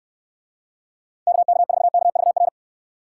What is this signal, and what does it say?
Morse code, heard as a keyed tone, about 69 wpm, 700 Hz. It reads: KK4KVR